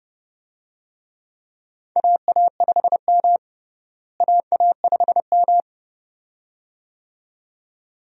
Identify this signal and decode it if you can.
Morse code, heard as a keyed tone, about 30 wpm, 700 Hz. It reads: AA5M AA5M
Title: USA Call Signs - 4 Characters - Repeated 2x - 30wpm